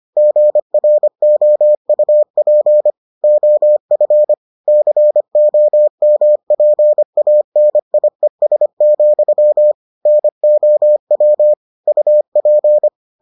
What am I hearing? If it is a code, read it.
Morse code, heard as a keyed tone, about 25 wpm, 600 Hz. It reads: GROUP OF COMPANIES, NOW UP